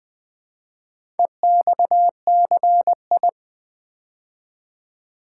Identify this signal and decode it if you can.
Morse code, heard as a keyed tone, about 20 wpm, 700 Hz. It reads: EXCI
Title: Build A Word - 4 Letters At A Time - 8-Letter Words - 20wpm